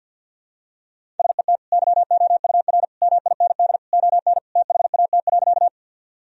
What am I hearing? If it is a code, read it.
Morse code, heard as a keyed tone, about 49 wpm, 700 Hz. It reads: SET YOUR MIND ON THAT.
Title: Sentences from Top 1000 Words - Review Part 1 - 50wpm